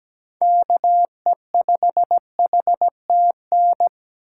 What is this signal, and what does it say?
Morse code, heard as a keyed tone, about 17 wpm, 700 Hz. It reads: KE5HTN
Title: USA Call Signs - 6 Characters - 3x Character Spacing - 17wpm